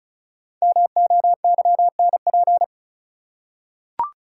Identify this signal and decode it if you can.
Morse code, heard as a keyed tone, about 36 wpm, 700 Hz. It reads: MOYNP
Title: Code Groups - 5 Letter:Numbers - 35wpm